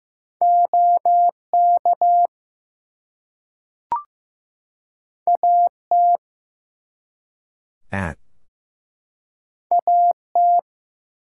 Morse code, heard as a keyed tone, about 15 wpm, 700 Hz. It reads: OK AT AT
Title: Two Letter Words - 15wpm